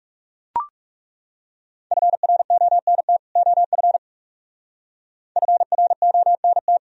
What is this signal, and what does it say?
Morse code, heard as a keyed tone, about 40 wpm, 700 Hz. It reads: FRONT OF FRONT